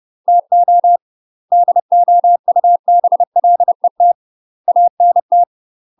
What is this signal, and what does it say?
Morse code, heard as a keyed tone, about 30 wpm, 700 Hz. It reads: TO DOUBLET ANT
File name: UK MASTER QSOs TRAINING / 30WPM / UK MASTER QSOs 0026